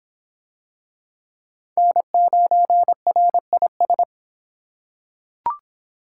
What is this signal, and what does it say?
Morse code, heard as a keyed tone, about 25 wpm, 700 Hz. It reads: N9RIS